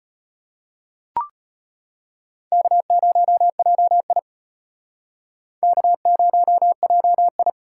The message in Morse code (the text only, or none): K0JI K0JI